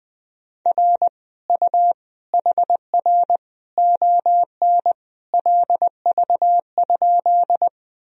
Morse code, heard as a keyed tone, about 21 wpm, 700 Hz. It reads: R U HR ON LV?
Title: Sentences from Top 400 Words and Abbreviations 20wpm